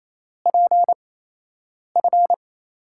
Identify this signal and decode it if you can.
Morse code, heard as a keyed tone, about 28 wpm, spaced as for 8 wpm, 700 Hz. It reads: PF
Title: USA Call Signs - 6 Characters - 8x Character Spacing - 28wpm